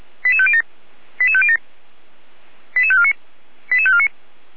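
Five-tone selective call (ZVEI), heard as quick runs of five tones, tones of about 70 ms. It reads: 88507 88507 88549 88549
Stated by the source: Received by SDR; ZVEI five-tone code 88427.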